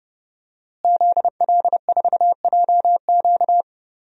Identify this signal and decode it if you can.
Morse code, heard as a keyed tone, about 32 wpm, 700 Hz. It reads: ZL4JQ
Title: DXCC-100 Call Signs - 5 Characters - Repeated 3x - 30wpm